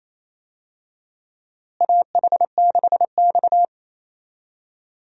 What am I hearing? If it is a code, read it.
Morse code, heard as a keyed tone, about 28 wpm, 700 Hz. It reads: AH6X
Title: USA Call Signs - 4 Characters - Repeated 2x - 28wpm